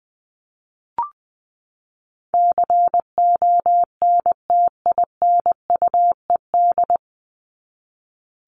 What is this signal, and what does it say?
Morse code, heard as a keyed tone, about 20 wpm, 700 Hz. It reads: CONTINUED
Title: ED Suffix - Top 100 Words with Base Words 20wpm